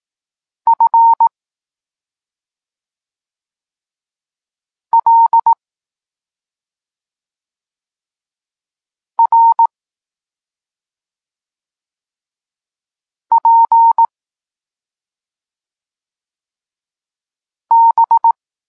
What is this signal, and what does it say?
Morse code, heard as a keyed tone, about 18 wpm, 920 Hz. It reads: F L R P B
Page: Tekst 32 – Doskonalenie odbioru poznanych liter